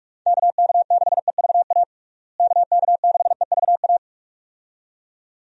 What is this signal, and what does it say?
Morse code, heard as a keyed tone, about 46 wpm, 700 Hz. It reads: KK6EVA KK6EVA
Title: USA Call Signs - 6 Characters - Repeated 2x - 45wpm